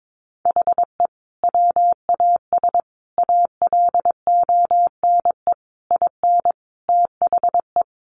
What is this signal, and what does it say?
Morse code, heard as a keyed tone, about 22 wpm, 700 Hz. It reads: HE WAS ALONE IN THE
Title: Sentences from Top 1000 Words 22wpm